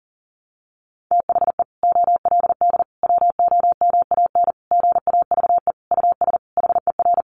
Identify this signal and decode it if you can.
Morse code, heard as a keyed tone, about 40 wpm, 700 Hz. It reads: THE OLD WOMAN GAVE US HER
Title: Sentences from Top 1100 Words 40wpm